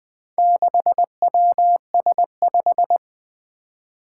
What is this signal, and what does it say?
Morse code, heard as a keyed tone, about 20 wpm, 700 Hz. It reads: 6WS5
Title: Four Challenging Letter-Number Combinations 20wpm